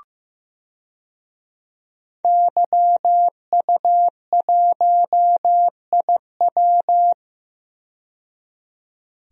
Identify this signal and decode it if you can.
Morse code, heard as a keyed tone, about 15 wpm, 700 Hz. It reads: YU1IW
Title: DXCC-100 Call Signs - 5 Characters - 15wpm